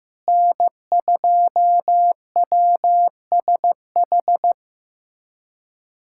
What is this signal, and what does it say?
Morse code, heard as a keyed tone, about 15 wpm, 700 Hz. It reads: N2WSH
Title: USA Callsign DE Callsign - 5 Characters - 15wpm